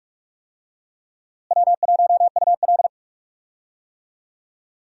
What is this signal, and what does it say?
Morse code, heard as a keyed tone, about 46 wpm, 700 Hz. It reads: W1UL
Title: USA Call Signs - 4 Characters - Repeated 3x - 45wpm